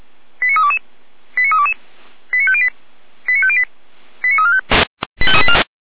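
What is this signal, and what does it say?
Five-tone selective call (ZVEI), heard as quick runs of five tones, tones of about 70 ms. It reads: 89322 89322 89508 89508 89345
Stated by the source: ZVEI five-tone selective call received by SDR, recorded by fire brigade feuerwehr-orlamuende, channel 455.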